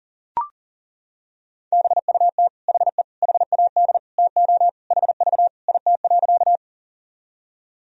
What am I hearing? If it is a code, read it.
Morse code, heard as a keyed tone, about 40 wpm, 700 Hz. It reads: BUT HE HAD TO HV IT.
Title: Sentences from Top 100 Words and Abbreviations 40wpm